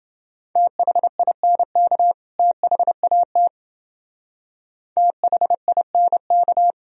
Morse code, heard as a keyed tone, about 28 wpm, 700 Hz. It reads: THINK THAT THINK